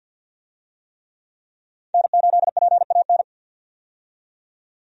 Morse code, heard as a keyed tone, about 52 wpm, 700 Hz. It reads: N8PAN
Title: USA Call Signs - 5 Characters - 50wpm